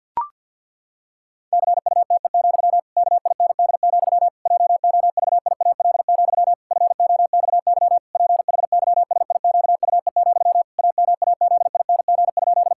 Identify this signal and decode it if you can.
Morse code, heard as a keyed tone, about 51 wpm, 700 Hz. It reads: CUTE, KIND, JOVIAL, FOXY PHYSIQUE, AMAZING?
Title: Pangram Challenge v2 - 50wpm